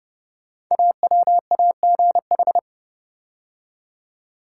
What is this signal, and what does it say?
Morse code, heard as a keyed tone, about 29 wpm, 700 Hz. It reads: AWAGH
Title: Code Groups - 5 Letter:Numbers - 30wpm